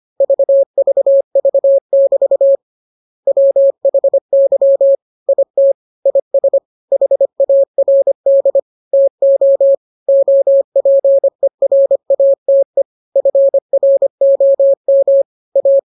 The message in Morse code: VVV<BT> WHY IT IS HARD TO OPERATE FROM A